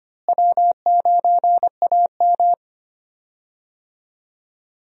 Morse code, heard as a keyed tone, about 25 wpm, 700 Hz. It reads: W9AM
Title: USA Call Signs - 4 Characters - Repeated 3x - 25wpm